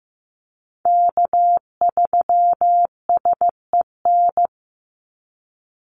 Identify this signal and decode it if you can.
Morse code, heard as a keyed tone, about 15 wpm, 700 Hz. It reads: K3SEN
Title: Call Signs Repeated 3x - 15wpm